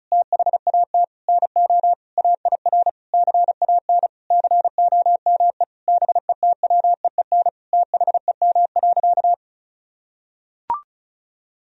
Morse code, heard as a keyed tone, about 36 wpm, 700 Hz. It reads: THAT NO AIR CAN COME BETWEEN THEM.
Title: Sentences from Top 1000 Words - Review Part 1 - 35wpm